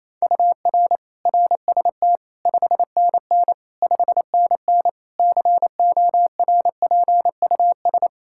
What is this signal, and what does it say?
Morse code, heard as a keyed tone, about 28 wpm, 700 Hz. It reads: UR RST 5NN 5NN CORPUS
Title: K1USN SST QSO Practice 28wpm